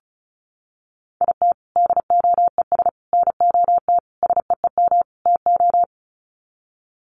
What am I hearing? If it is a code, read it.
Morse code, heard as a keyed tone, about 34 wpm, 700 Hz. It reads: IT DOES NOT SEEM TO
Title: Sets of 5 Words Encore2 35wpm